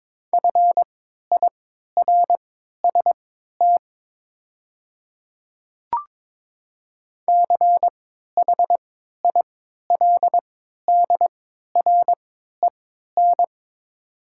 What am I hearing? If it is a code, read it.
Morse code, heard as a keyed tone, about 22 wpm, spaced as for 12 wpm, 700 Hz. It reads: FIRST CHILDREN
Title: 3x Character Spacing - Top 300 Words - 22wpm